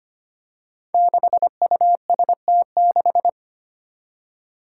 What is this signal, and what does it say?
Morse code, heard as a keyed tone, about 25 wpm, 700 Hz. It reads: 6UST6
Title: Five Challenging Letter-Number Combinations 25wpm